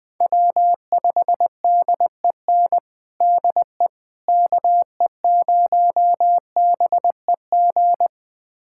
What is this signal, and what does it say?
Morse code, heard as a keyed tone, about 20 wpm, 700 Hz. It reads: W5DEN DE KE0BEG
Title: USA Callsign DE Callsign 20wpm